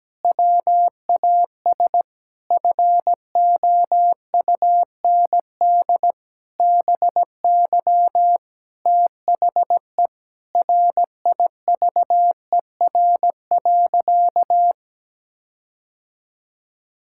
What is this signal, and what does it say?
Morse code, heard as a keyed tone, about 17 wpm, 700 Hz. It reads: WAS FOUND BY THE RIVER.
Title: Sentences from Top 1600 Words 17wpm